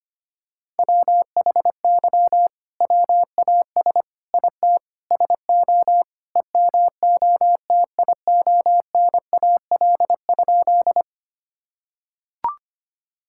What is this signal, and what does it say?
Morse code, heard as a keyed tone, about 25 wpm, 700 Hz. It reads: WHY WAS IT SO EMOTIONAL?
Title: Sentences from Top 1000 Words - Review Part 5 - 25wpm